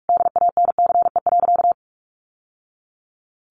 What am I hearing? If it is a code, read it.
Morse code, heard as a keyed tone, about 44 wpm, 700 Hz. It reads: DANCE.
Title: Sentences from Top 900 Words 45wpm